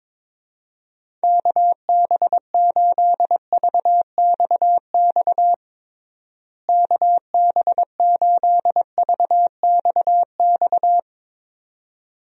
Morse code, heard as a keyed tone, about 21 wpm, 700 Hz. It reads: KB8VXX KB8VXX
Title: USA Call Signs - 6 Characters - Repeated 2x - 22wpm